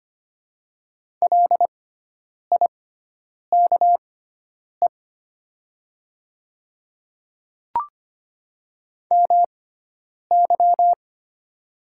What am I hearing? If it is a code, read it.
Morse code, heard as a keyed tone, about 25 wpm, spaced as for 9 wpm, 700 Hz. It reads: LIKE MY
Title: 6x Character Spacing - Top 100 Words - 25wpm